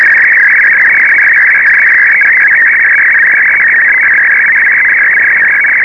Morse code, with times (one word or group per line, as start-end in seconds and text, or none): none